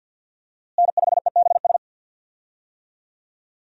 Morse code, heard as a keyed tone, about 50 wpm, 700 Hz. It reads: N5EBS
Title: USA Call Signs - 5 Characters - Repeated 3x - 50wpm